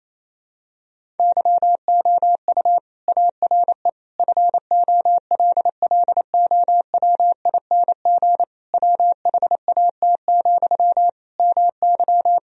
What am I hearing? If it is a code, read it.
Morse code, heard as a keyed tone, about 28 wpm, 700 Hz. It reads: YOU ARE FOLLOWING WHAT, MY
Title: Sentences from Top 1000 Words - Review Part 4 - 28wpm